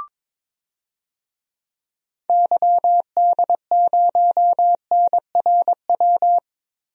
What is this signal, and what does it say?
Morse code, heard as a keyed tone, about 22 wpm, 700 Hz. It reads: YD0NRW
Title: DXCC-100 Call Signs - 6 Characters - Repeated 3x - 22wpm